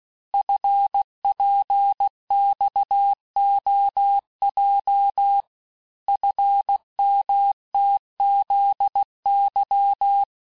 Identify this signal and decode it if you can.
Morse code, heard as a keyed tone, about 16 wpm, 800 Hz. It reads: FPXOJ FMTZY
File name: cours-du-lundi files / Cours-1586-texte-chiffre-vitesse-15mpm